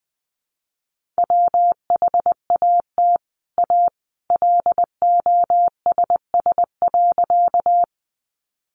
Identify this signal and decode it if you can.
Morse code, heard as a keyed tone, about 20 wpm, 700 Hz. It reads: WHAT A LOSS.
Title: Sentences from Top 1400 Words 20wpm